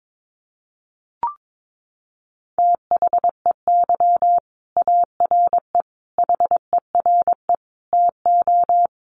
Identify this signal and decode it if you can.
Morse code, heard as a keyed tone, about 22 wpm, 700 Hz. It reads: THEY ARE HERE TO
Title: Sentences from Top 1500 Words 22wpm